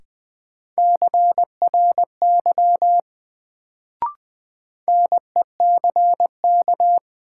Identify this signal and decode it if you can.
Morse code, heard as a keyed tone, about 20 wpm, 700 Hz. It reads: CRY NECK
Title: Sets of 5 Words Encore - New Word Drill - 20wpm